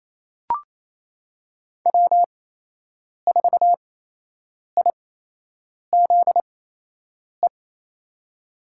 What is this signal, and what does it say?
Morse code, heard as a keyed tone, about 28 wpm, spaced as for 8 wpm, 700 Hz. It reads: W4IZE
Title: USA Call Signs - 5 Characters - 8x Character Spacing - 28wpm